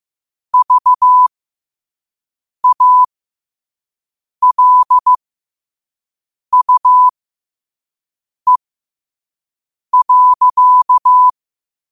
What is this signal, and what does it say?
Morse code, heard as a keyed tone, about 15 wpm, spaced as for 5 wpm, 1000 Hz. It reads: VALUE.